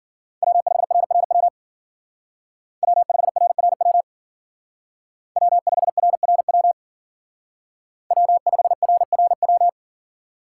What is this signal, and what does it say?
Morse code, heard as a keyed tone, about 52 wpm, 700 Hz. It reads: W5RRW W5RRW W5RRW W5RRW